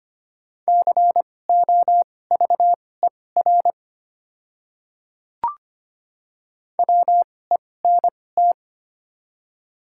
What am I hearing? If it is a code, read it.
Morse code, heard as a keyed tone, about 25 wpm, spaced as for 18 wpm, 700 Hz. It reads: COVER WENT
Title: 2x Character Spacing - Top 300 Words - 25wpm